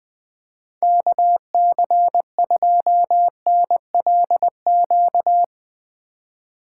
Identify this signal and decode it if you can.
Morse code, heard as a keyed tone, about 20 wpm, 700 Hz. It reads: KC2NLQ
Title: Speed Racing - USA 2x3 Call Signs 20wpm